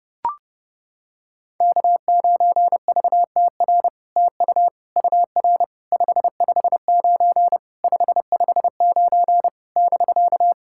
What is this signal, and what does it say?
Morse code, heard as a keyed tone, about 30 wpm, 700 Hz. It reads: K9VTR TU UR 559 559 <BK>
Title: SOTA QSO Practice 30wpm